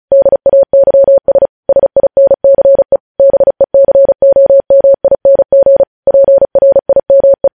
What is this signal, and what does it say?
Morse code, heard as a keyed tone, about 36 wpm, 550 Hz. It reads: DAYS SINCE BECOMING PRIME